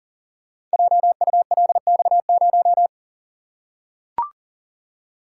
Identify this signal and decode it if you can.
Morse code, heard as a keyed tone, about 40 wpm, 700 Hz. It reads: JULX0
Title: Code Groups - 5 Letter:Numbers - 40wpm